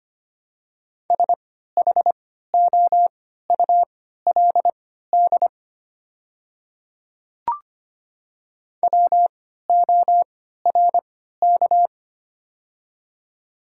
Morse code, heard as a keyed tone, about 25 wpm, spaced as for 14 wpm, 690 Hz. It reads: SHOULD WORK
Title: 3x Character Spacing - Top 300 Words - 25wpm